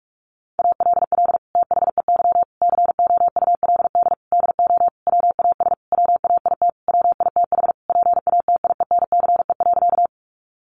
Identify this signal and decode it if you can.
Morse code, heard as a keyed tone, about 44 wpm, 700 Hz. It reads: ALL THEY COULD DO WAS WAIT WITH PATIENCE.
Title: Sentences from Top 1300 Words 45wpm